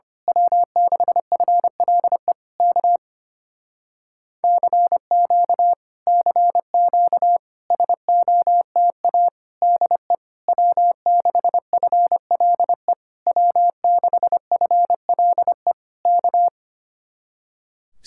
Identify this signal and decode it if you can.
Morse code, heard as a keyed tone, about 24 wpm, 700 Hz. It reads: W6FLE K CQ CQ SOTA DE W6FLE W6FLE K
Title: Speed-Racing Common QSO Phrases Encore - 25wpm